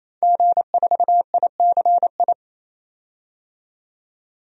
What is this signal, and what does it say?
Morse code, heard as a keyed tone, about 28 wpm, 700 Hz. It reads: G4ICI
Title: DXCC-100 Call Signs - 5 Characters - 28wpm